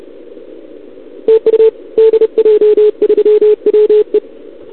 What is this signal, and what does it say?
Morse code, heard as a keyed tone, about 30 wpm, 440 Hz. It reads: TU DJ3WE
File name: CQWWCW2021 / Doderna OL3Z